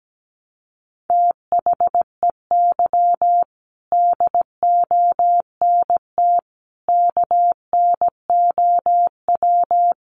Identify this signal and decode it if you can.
Morse code, heard as a keyed tone, about 17 wpm, 700 Hz. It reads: THEY DONT KNOW
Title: Sentences from Top 300 Words 17wpm